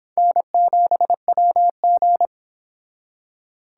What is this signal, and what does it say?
Morse code, heard as a keyed tone, about 23 wpm, 700 Hz. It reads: N7WG